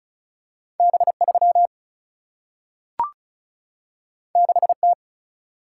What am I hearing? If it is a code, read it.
Morse code, heard as a keyed tone, about 36 wpm, 700 Hz. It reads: B3 6T
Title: Single-Letter Single-Number 35wpm